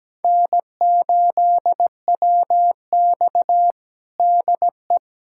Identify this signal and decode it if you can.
Morse code, heard as a keyed tone, about 17 wpm, 700 Hz. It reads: N8WX DE
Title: USA Callsign DE Callsign - 4 Characters - 17wpm